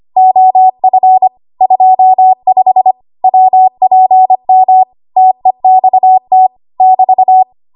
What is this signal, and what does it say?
Morse code, heard as a keyed tone, about 25 wpm, 750 Hz. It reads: OF 25 WPM TEXT <BT>
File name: Archive / 25 WPM / 25WPM